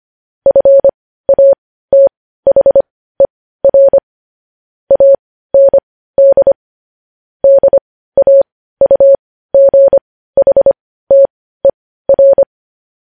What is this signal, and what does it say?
Morse code, heard as a keyed tone, about 25 wpm, spaced as for 15 wpm, 550 Hz. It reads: FATHER AND DAUGHTER
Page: news-headlines-in-morse-code-at-15-wpm